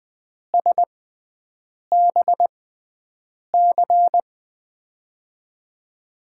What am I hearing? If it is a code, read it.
Morse code, heard as a keyed tone, about 20 wpm, spaced as for 7 wpm, 700 Hz. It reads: SBC
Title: USA Call Signs - 5 Characters - 6x Character Spacing - 20wpm